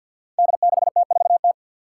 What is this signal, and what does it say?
Morse code, heard as a keyed tone, about 50 wpm, 700 Hz. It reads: D6T4T
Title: Five Challenging Letter-Number Combinations 50wpm